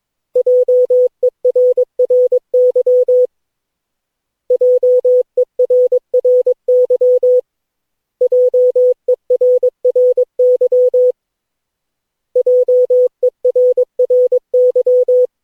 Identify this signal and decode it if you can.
Morse code, heard as a keyed tone, about 22 wpm, 500 Hz. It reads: JERRY JERRY JERRY JERRY
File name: cw / Session9 cw training